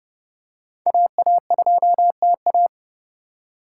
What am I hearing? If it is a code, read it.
Morse code, heard as a keyed tone, about 31 wpm, 700 Hz. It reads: AA2TA
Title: USA Call Signs - 8x Character Spacing - 30wpm